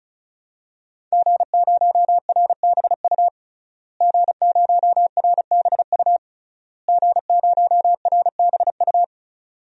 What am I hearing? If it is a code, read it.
Morse code, heard as a keyed tone, about 36 wpm, 700 Hz. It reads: G0RBU G0RBU G0RBU
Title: DXCC-100 Call Signs - 5 Characters - Repeated 3x - 35wpm